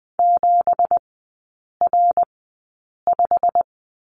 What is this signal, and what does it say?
Morse code, heard as a keyed tone, about 20 wpm, spaced as for 8 wpm, 700 Hz. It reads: 7R5
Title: 191 Lesson 31e - Letter v - Three Characters 20wpm